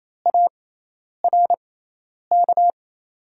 Morse code, heard as a keyed tone, about 28 wpm, spaced as for 10 wpm, 700 Hz. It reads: ARK
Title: 6x Character Spacing - Top 300 Words - 28wpm